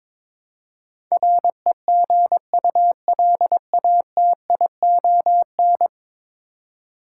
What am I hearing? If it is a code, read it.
Morse code, heard as a keyed tone, about 22 wpm, 700 Hz. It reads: REGULATION
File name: practice / 22wpm / 10 Letter Words - First 3 Letters Given - 22wpm